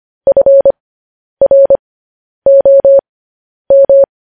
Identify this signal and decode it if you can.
Morse code, heard as a keyed tone, about 25 wpm, spaced as for 10 wpm, 550 Hz. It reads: FROM
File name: podcast / News headlines-1x-25x10